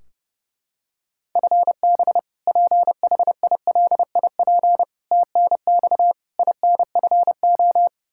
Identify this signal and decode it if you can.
Morse code, heard as a keyed tone, about 30 wpm, 700 Hz. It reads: FB PHILIP TNX INFO
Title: Practice QSO - Exchange 3 - More Info - 30wpm